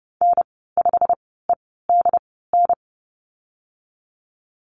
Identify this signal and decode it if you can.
Morse code, heard as a keyed tone, about 30 wpm, spaced as for 17 wpm, 690 Hz. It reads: N5EDN
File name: Lesson 15g - Number 5 - Call Signs 30wpm